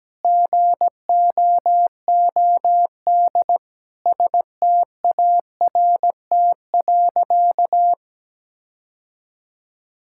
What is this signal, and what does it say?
Morse code, heard as a keyed tone, about 17 wpm, 700 Hz. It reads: GOOD START.
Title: Sentences from Top 1600 Words 17wpm